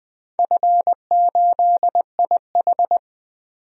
Morse code, heard as a keyed tone, about 20 wpm, 700 Hz. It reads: F8IH
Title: DXCC-100 Call Signs - 4 Characters - 20wpm